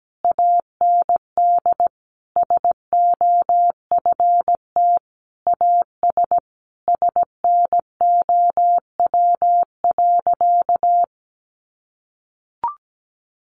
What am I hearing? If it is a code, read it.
Morse code, heard as a keyed tone, about 17 wpm, 700 Hz. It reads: AND SOFT AS SNOW.
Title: Sentences from Top 1500 Words 17wpm